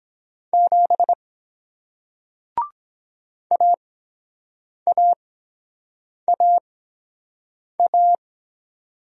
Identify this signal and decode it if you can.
Morse code, heard as a keyed tone, about 23 wpm, spaced as for 7 wpm, 700 Hz. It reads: 7 AAAA